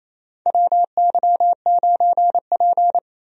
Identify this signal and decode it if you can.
Morse code, heard as a keyed tone, about 28 wpm, 700 Hz. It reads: WY9P
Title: USA Call Signs - 4 Characters - 28wpm